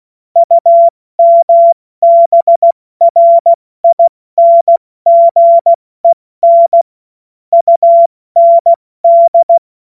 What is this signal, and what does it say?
Morse code, heard as a keyed tone, about 16 wpm, 680 Hz. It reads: UMBRINGEN UND
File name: Morsekurs / Witze / Witz 02 April 2024 70 F80